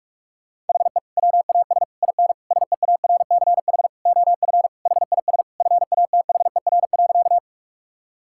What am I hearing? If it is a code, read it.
Morse code, heard as a keyed tone, about 41 wpm, 700 Hz. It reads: HE WAS IN SEARCH OF HIS FATHER.
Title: Sentences from Top 1000 Words - Review Part 5 - 45wpm